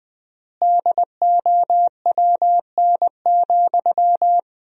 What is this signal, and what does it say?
Morse code, heard as a keyed tone, about 20 wpm, 700 Hz. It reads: DOWN,
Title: 4x Word-Spacing - Sentences from Top 500 Words - 20wpm